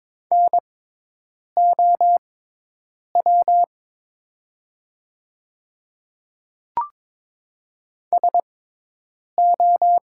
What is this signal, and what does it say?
Morse code, heard as a keyed tone, about 22 wpm, spaced as for 8 wpm, 700 Hz. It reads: NOW SO
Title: 6x Character Spacing - Top 100 Words - 22wpm